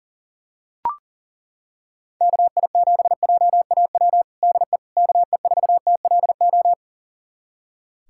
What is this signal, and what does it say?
Morse code, heard as a keyed tone, about 40 wpm, 700 Hz. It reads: KI7JAW DE KE4TLO
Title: USA Callsign DE Callsign 40wpm